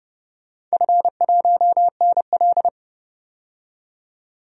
Morse code, heard as a keyed tone, about 32 wpm, 700 Hz. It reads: F1NL
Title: DXCC-100 Call Signs - 4 Characters - 30wpm